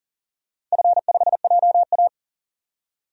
Morse code, heard as a keyed tone, about 40 wpm, 700 Hz. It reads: F5JA
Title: DXCC-100 Call Signs - 4 Characters - 40wpm